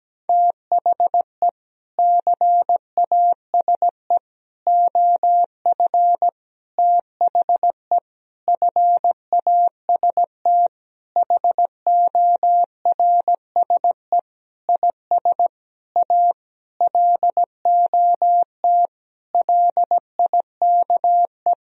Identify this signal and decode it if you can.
Morse code, heard as a keyed tone, about 17 wpm, 700 Hz. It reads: THE CASE OF THE FAST HORSE IS A LOT LIKE
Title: Sentences from Top 1000 Words - Review Part 2 - 17wpm